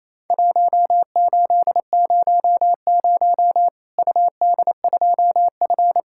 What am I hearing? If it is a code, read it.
Morse code, heard as a keyed tone, about 28 wpm, 700 Hz. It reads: 1800 UD2F
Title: Worked All Europe DX Contest - QTC Exchange - 28wpm